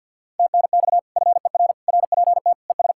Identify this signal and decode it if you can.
Morse code, heard as a keyed tone, about 64 wpm, 700 Hz. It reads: TNX FER RPT ES